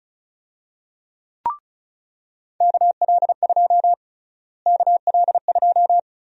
Morse code, heard as a keyed tone, about 34 wpm, 700 Hz. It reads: KL2 KL2
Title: USA Call Sign Prefixes - Repeated 2x - 35wpm